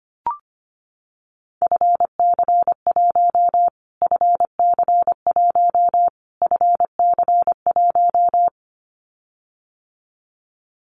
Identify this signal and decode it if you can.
Morse code, heard as a keyed tone, about 25 wpm, 700 Hz. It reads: FC1 FC1 FC1
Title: Callsign Prefixes for Top 25 DXCC - Repeated 3x - 25wpm